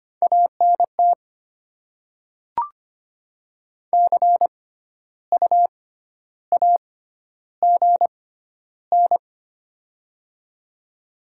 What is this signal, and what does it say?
Morse code, heard as a keyed tone, about 25 wpm, spaced as for 12 wpm, 700 Hz. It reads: ANT CU AGN
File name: Top 20 QSO Elements - 6x Character Spacing - 25wpm